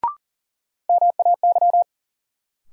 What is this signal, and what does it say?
Morse code, heard as a keyed tone, about 38 wpm, 700 Hz. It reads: MAY
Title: Sentences from Top 100 Words - New Word Drill - 40wpm